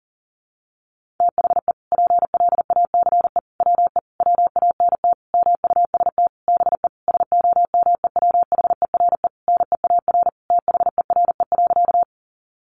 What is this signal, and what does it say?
Morse code, heard as a keyed tone, about 40 wpm, 700 Hz. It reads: THE PLACE WE WANT MUST BE SOMEWHERE NEAR THERE.
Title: Sentences from Top 1400 Words 40wpm